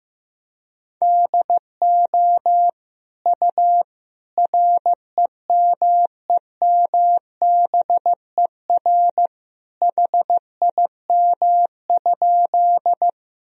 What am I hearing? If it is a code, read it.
Morse code, heard as a keyed tone, about 16 wpm, 700 Hz. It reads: DO U REMEMBER HIM?
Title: Sentences from Top 400 Words and Abbreviations 15wpm